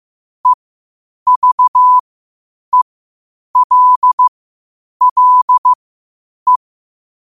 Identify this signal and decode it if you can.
Morse code, heard as a keyed tone, about 15 wpm, spaced as for 8 wpm, 1000 Hz. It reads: EVELLE
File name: qotdfiles / 8wpm